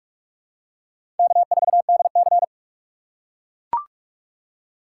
Morse code, heard as a keyed tone, about 42 wpm, 700 Hz. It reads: K4DC
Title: USA Call Signs - 2x Character Spacing - 45wpm